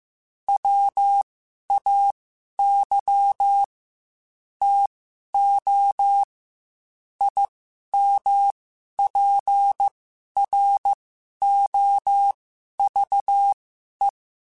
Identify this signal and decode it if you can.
Morse code, heard as a keyed tone, about 15 wpm, spaced as for 11 wpm, 800 Hz. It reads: WAY TO IMPROVE
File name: Quotes / Quotes-mp3s / A Smile @13